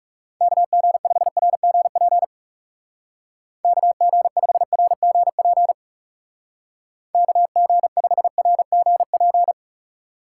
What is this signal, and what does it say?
Morse code, heard as a keyed tone, about 37 wpm, 700 Hz. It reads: KG5RGP KG5RGP KG5RGP
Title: Speed Racing - USA 2x3 Call Signs 30wpm